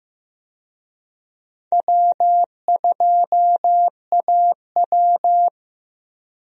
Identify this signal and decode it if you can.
Morse code, heard as a keyed tone, about 15 wpm, 700 Hz. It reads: W2AW
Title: USA Call Signs - 4 Characters - Repeated 3x - 15wpm